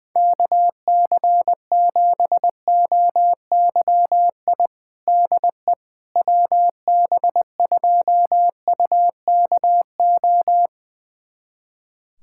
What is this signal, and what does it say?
Morse code, heard as a keyed tone, about 20 wpm, 700 Hz. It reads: KC7OYI DE WB2UKO
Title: USA Callsign DE Callsign - 6 Characters - 20wpm